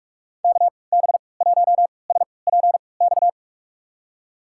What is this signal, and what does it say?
Morse code, heard as a keyed tone, about 46 wpm, spaced as for 25 wpm, 700 Hz. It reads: KB1SPX
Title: USA Call Signs - 6 Characters - 3x Character Spacing - 45wpm